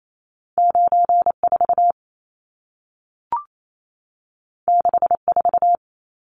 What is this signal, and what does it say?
Morse code, heard as a keyed tone, about 28 wpm, 700 Hz. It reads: 94 64
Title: Numbers - 2-Digit 28wpm